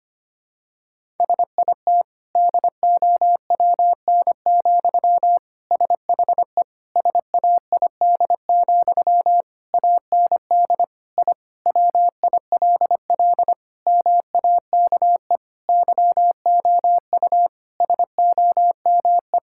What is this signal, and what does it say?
Morse code, heard as a keyed tone, about 25 wpm, 700 Hz. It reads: SIT DOWN, SHE SAID, AND I WILL MAKE YOU SOME
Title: Sentences from Top 1000 Words - Review Part 6 - 25wpm